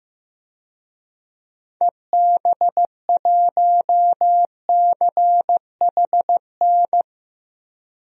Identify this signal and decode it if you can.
Morse code, heard as a keyed tone, about 15 wpm, 700 Hz. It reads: EB1CHN
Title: DXCC-100 Call Signs - 6 Characters - 15wpm